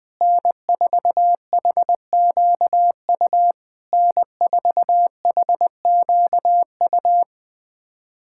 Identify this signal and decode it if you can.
Morse code, heard as a keyed tone, about 20 wpm, 690 Hz. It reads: N4HQU N4HQU
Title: SOTA QSO Practice 20wpm